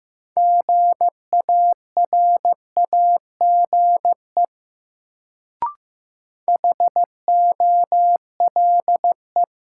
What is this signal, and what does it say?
Morse code, heard as a keyed tone, about 16 wpm, 700 Hz. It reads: GARAGE HOLE
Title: Sentences from Top 1600 Words - New Word Drill - 15wpm